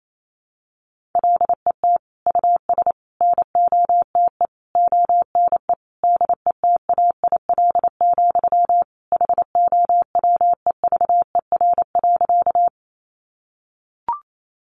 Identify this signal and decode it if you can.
Morse code, heard as a keyed tone, about 28 wpm, 700 Hz. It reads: LET US NOTE ONE DETAIL, HOWEVER.
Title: Sentences from Top 1500 Words 28wpm